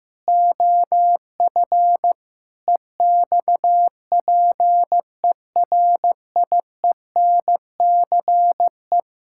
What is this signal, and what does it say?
Morse code, heard as a keyed tone, about 16 wpm, 700 Hz. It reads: OF EXPERIENCE
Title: Sentences from Top 1000 Words - Review Part 6 - 15wpm